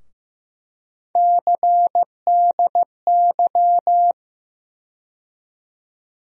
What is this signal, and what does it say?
Morse code, heard as a keyed tone, about 15 wpm, 700 Hz. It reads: CDY
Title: USA Call Sign Postfixes - 3 Letters 15wpm